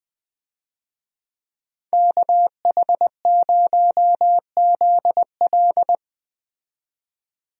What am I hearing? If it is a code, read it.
Morse code, heard as a keyed tone, about 20 wpm, 700 Hz. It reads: KH0ZL
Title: USA Call Signs - 5 Characters - 20wpm